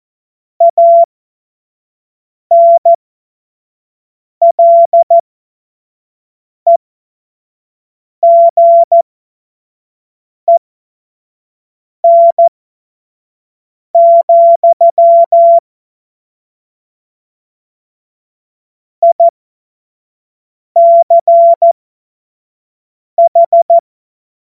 Morse code, heard as a keyed tone, about 14 wpm, spaced as for 5 wpm, 680 Hz. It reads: ANLEGEN, ICH
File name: Morsekurs / Witze / Witz 11. April 2023 25 F70